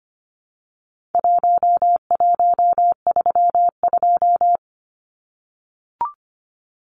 Morse code, heard as a keyed tone, about 25 wpm, 700 Hz. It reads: 1132
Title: Numbers - Contest Style - 4-Digit 25wpm